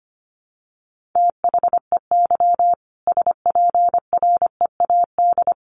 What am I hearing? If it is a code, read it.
Morse code, heard as a keyed tone, about 25 wpm, 700 Hz. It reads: THEY SPREAD
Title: Sentences from Top 1500 Words 25wpm